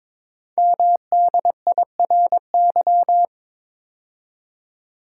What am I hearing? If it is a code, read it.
Morse code, heard as a keyed tone, about 22 wpm, 690 Hz. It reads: MDIRY
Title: Code Groups - 5 Letters - 22wpm